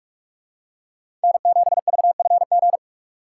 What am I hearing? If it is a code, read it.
Morse code, heard as a keyed tone, about 45 wpm, 700 Hz. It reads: N7VFG